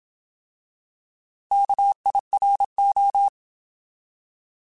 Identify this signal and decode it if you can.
Morse code, heard as a keyed tone, about 27 wpm, 800 Hz. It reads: KIRO
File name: Alphabetical / Alphabetical mp3s / Ks @25